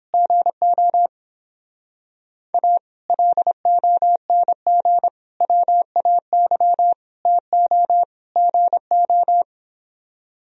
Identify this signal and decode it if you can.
Morse code, heard as a keyed tone, about 29 wpm, 700 Hz. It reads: GO A LONG WAY TO GO